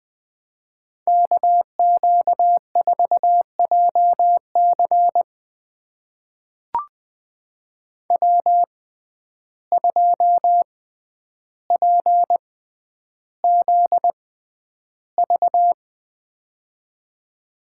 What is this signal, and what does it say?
Morse code, heard as a keyed tone, about 20 wpm, spaced as for 10 wpm, 700 Hz. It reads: KQ4JC W2PZV
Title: USA Call Signs - 5 Characters - 6x Character Spacing - 20wpm